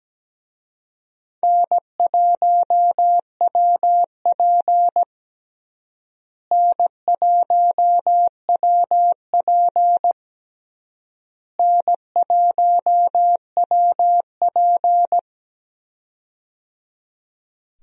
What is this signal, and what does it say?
Morse code, heard as a keyed tone, about 17 wpm, 700 Hz. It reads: N1WP N1WP N1WP
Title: USA Call Signs - 4 Characters - Repeated 3x - 17wpm